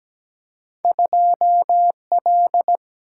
Morse code, heard as a keyed tone, about 17 wpm, 700 Hz. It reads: 2 L
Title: Single-Letter Single-Number 17wpm